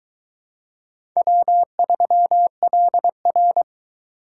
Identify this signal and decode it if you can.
Morse code, heard as a keyed tone, about 23 wpm, 700 Hz. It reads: W3LR